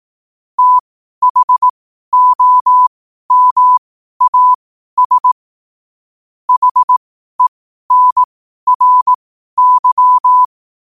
Morse code, heard as a keyed tone, about 18 wpm, spaced as for 12 wpm, 1000 Hz. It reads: THOMAS HENRY
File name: qotdfiles / 12wpm